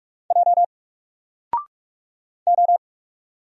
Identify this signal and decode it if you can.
Morse code, heard as a keyed tone, about 44 wpm, 700 Hz. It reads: J O
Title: Single Character (Letter:Number:Punctuation:Prosign) 45wpm